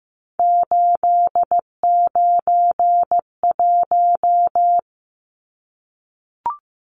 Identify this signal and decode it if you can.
Morse code, heard as a keyed tone, about 15 wpm, 700 Hz. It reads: 891
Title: Numbers - 3-Digit 15wpm